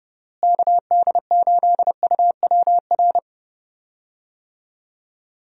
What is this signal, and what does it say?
Morse code, heard as a keyed tone, about 31 wpm, 700 Hz. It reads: KD8UWR
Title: USA Call Signs - 6 Characters - Repeated 2x - 30wpm